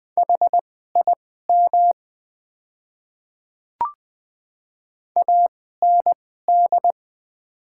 Morse code, heard as a keyed tone, about 20 wpm, spaced as for 15 wpm, 700 Hz. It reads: HIM AND
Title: 2x Character Spacing - Top 100 Words - 20wpm